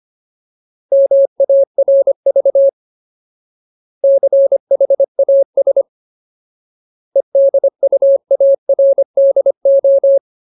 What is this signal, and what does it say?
Morse code, heard as a keyed tone, about 25 wpm, 550 Hz. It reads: MARV CHAS EDUARDO
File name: Ultimate Word List / 25WPM Ultimate